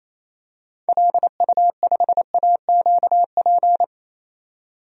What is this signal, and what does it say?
Morse code, heard as a keyed tone, about 28 wpm, 700 Hz. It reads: LU5AQP
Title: DXCC-100 Call Signs - 6 Characters - 28wpm